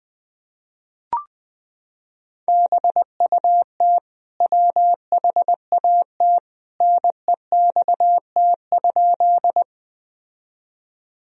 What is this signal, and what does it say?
Morse code, heard as a keyed tone, about 21 wpm, 700 Hz. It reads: BUT WHAT NEXT?
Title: Sentences from Top 1000 Words - Review Part 1 - 20wpm